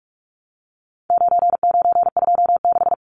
Morse code, heard as a keyed tone, about 44 wpm, 700 Hz. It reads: 9926
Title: Numbers - 4-Digit 45wpm